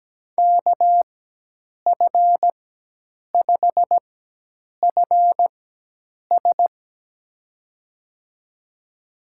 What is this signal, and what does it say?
Morse code, heard as a keyed tone, about 17 wpm, spaced as for 8 wpm, 700 Hz. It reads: KF5FS
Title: USA Call Signs - 5 Characters - 4x Character Spacing - 17wpm